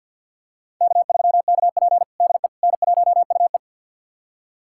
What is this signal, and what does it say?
Morse code, heard as a keyed tone, about 50 wpm, 700 Hz. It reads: K3KP DE N1UE
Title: USA Callsign DE Callsign - 4 Characters - 50wpm